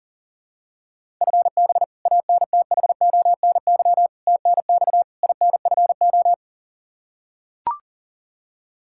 Morse code, heard as a keyed tone, about 40 wpm, 700 Hz. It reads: FB ANTHONY TNX INFO
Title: K1USN SST QSO Practice 40wpm